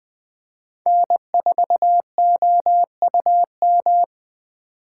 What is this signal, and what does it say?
Morse code, heard as a keyed tone, about 20 wpm, 700 Hz. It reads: N4OUM
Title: USA Call Signs - 5 Characters - 3x Character Spacing - 20wpm